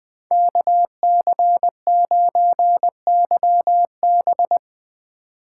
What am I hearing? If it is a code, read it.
Morse code, heard as a keyed tone, about 22 wpm, 700 Hz. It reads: KC9YB
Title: Speed Racing - USA 2x2 Call Signs 20wpm